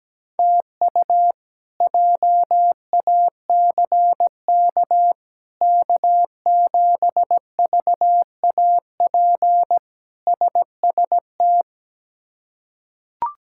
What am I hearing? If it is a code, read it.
Morse code, heard as a keyed tone, about 17 wpm, 700 Hz. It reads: TU JACK K7VAP SST
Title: K1USN SST QSO Practice - 17wpm